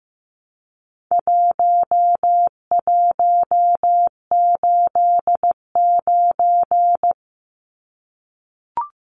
Morse code, heard as a keyed tone, about 15 wpm, 700 Hz. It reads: 1189
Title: Numbers - 4-Digit 15wpm